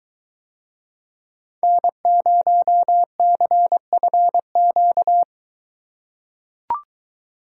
Morse code, heard as a keyed tone, about 22 wpm, 700 Hz. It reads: N0CFQ